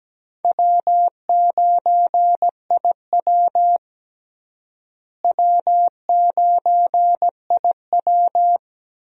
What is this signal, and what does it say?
Morse code, heard as a keyed tone, about 17 wpm, 700 Hz. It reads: W9IW W9IW
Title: USA Call Signs - 4 Characters - Repeated 2x - 17wpm